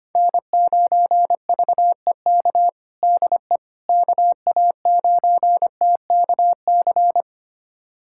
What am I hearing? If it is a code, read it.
Morse code, heard as a keyed tone, about 25 wpm, 700 Hz. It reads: N9VEK DE KA9TKC
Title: USA Callsign DE Callsign 25wpm